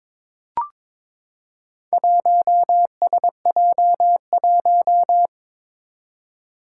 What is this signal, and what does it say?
Morse code, heard as a keyed tone, about 22 wpm, 700 Hz. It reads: 1SJ1
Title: Four Challenging Letter-Number Combinations 22wpm